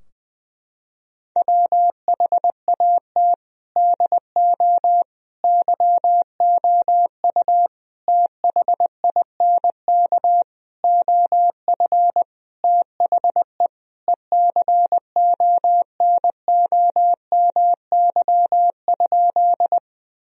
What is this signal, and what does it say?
Morse code, heard as a keyed tone, about 20 wpm, 700 Hz. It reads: WHAT DO YOU THINK OF THE ECONOMY?
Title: Sentences from Top 1000 Words - Review Part 3 - 20wpm